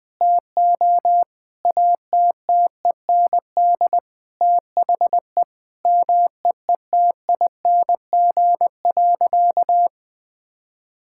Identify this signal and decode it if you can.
Morse code, heard as a keyed tone, about 20 wpm, 700 Hz. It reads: TO ATTEND THE MEETING.
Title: Sentences with 5-Word Phrases - 20wpm